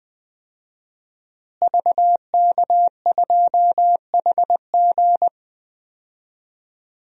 Morse code, heard as a keyed tone, about 20 wpm, 700 Hz. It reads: VK2HG
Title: DXCC-100 Call Signs - 5 Characters - Repeated 3x - 20wpm